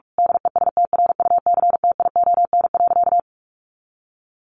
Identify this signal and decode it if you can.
Morse code, heard as a keyed tone, about 44 wpm, 700 Hz. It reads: DESTRUCTION.
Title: A Sound Of Thunder 45wpm